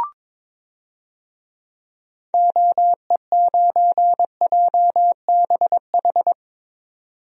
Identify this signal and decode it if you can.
Morse code, heard as a keyed tone, about 22 wpm, 700 Hz. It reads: OE9JBH
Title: DXCC-100 Call Signs - 6 Characters - 22wpm